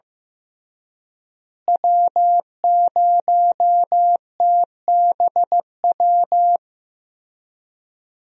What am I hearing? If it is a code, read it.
Morse code, heard as a keyed tone, about 15 wpm, 700 Hz. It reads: W0TBW
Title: USA Call Signs - 5 Characters - Repeated 3x - 15wpm